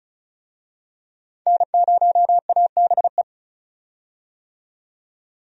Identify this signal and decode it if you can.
Morse code, heard as a keyed tone, about 36 wpm, 700 Hz. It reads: N0ABE
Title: USA Call Signs - 5 Characters - 35wpm